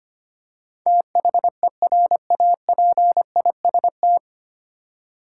Morse code, heard as a keyed tone, about 25 wpm, 690 Hz. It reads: THERAPIST
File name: practice / 25wpm / 9 Letter Words - First 3 Letters Given - 25wpm